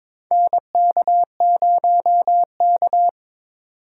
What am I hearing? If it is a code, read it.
Morse code, heard as a keyed tone, about 22 wpm, 700 Hz. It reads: NK0K
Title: USA Call Signs - 4 Characters - 4x Character Spacing - 22wpm